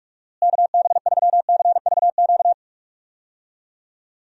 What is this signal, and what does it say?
Morse code, heard as a keyed tone, about 46 wpm, 700 Hz. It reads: KB3CVQ
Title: USA Call Signs - 6 Characters - Repeated 3x - 45wpm